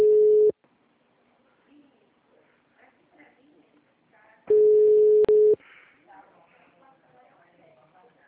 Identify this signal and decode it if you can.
Ringback tone (Europe, 425 Hz), with speech.
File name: out- -2020- - - .15463